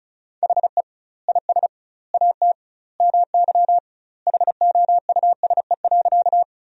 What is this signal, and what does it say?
Morse code, heard as a keyed tone, about 36 wpm, 700 Hz. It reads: HE IS AT MY HOUSE.
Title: 2x Word-Spacing - Sentences from Top 200 Words - 35wpm